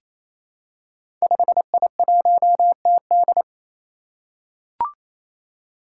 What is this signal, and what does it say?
Morse code, heard as a keyed tone, about 27 wpm, 700 Hz. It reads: 5I1TD
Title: Five Challenging Letter-Number Combinations 28wpm